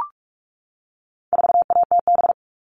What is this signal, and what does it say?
Morse code, heard as a keyed tone, about 44 wpm, 700 Hz. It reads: 4ATB
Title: All Cut Numbers - 4-Digit 45wpm